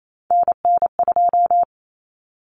Morse code, heard as a keyed tone, about 28 wpm, 700 Hz. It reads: NN2
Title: Common Cut Numbers - 3-Digit 28wpm